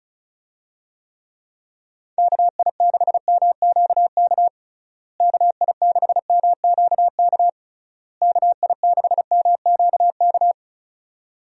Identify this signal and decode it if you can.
Morse code, heard as a keyed tone, about 36 wpm, 700 Hz. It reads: KI6MQK KI6MQK KI6MQK
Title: USA Call Signs - 6 Characters - Repeated 3x - 35wpm